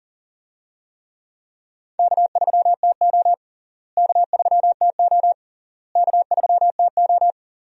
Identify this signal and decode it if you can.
Morse code, heard as a keyed tone, about 40 wpm, 700 Hz. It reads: K3TO K3TO K3TO
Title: USA Call Signs - 4 Characters - Repeated 3x - 40wpm